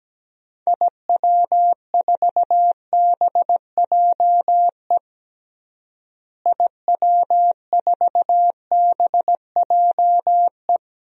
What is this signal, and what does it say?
Morse code, heard as a keyed tone, about 17 wpm, 700 Hz. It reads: IW4BJE IW4BJE
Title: DXCC-100 Call Signs - 6 Characters - Repeated 2x - 17wpm